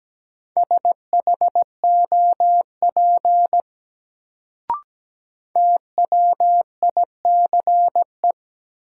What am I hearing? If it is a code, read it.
Morse code, heard as a keyed tone, about 17 wpm, 700 Hz. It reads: SHOP TWICE
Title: Sentences from Top 1600 Words - New Word Drill - 17wpm